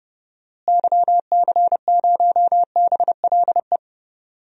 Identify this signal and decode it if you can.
Morse code, heard as a keyed tone, about 29 wpm, 700 Hz. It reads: YC0BLE
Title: DXCC-100 Call Signs - 6 Characters - Repeated 3x - 30wpm